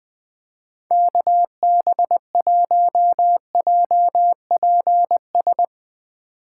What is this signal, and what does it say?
Morse code, heard as a keyed tone, about 22 wpm, 700 Hz. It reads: KB1JPS
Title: Speed Racing - USA 2x3 Call Signs 20wpm